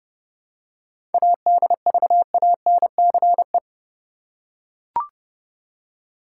Morse code, heard as a keyed tone, about 30 wpm, 700 Hz. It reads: ADVANCE
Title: Seven Letter Words - 30wpm